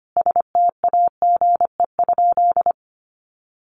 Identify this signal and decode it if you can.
Morse code, heard as a keyed tone, about 25 wpm, 700 Hz. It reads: STAGE?
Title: Sentences from Top 1200 Words 25wpm